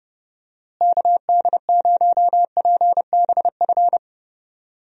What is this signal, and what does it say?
Morse code, heard as a keyed tone, about 30 wpm, 700 Hz. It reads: KD0PBF
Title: USA Call Signs - 6 Characters - Repeated 3x - 30wpm